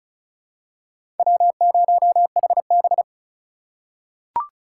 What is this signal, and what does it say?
Morse code, heard as a keyed tone, about 36 wpm, 700 Hz. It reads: W0HB
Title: USA Call Signs - 4 Characters - 2x Character Spacing - 35wpm